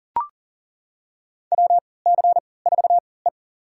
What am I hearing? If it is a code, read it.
Morse code, heard as a keyed tone, about 40 wpm, spaced as for 23 wpm, 700 Hz. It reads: WC4E
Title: USA Call Signs - 4 Characters - 3x Character Spacing - 40wpm